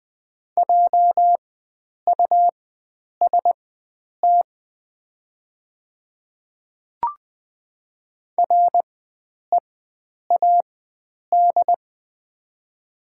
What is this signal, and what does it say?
Morse code, heard as a keyed tone, about 20 wpm, spaced as for 9 wpm, 700 Hz. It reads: JUST READ
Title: 4x Character Spacing - Top 300 Words - 20wpm